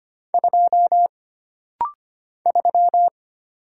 Morse code, heard as a keyed tone, about 25 wpm, 700 Hz. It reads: 2 3
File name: Single Character (Letter:Number:Punctuation:Prosign) - Rapid-Fire - 25wpm